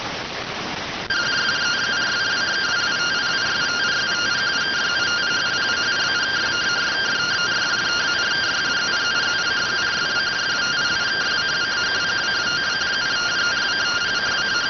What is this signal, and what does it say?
Two-tone teleprinter signal, about 170 Hz shift.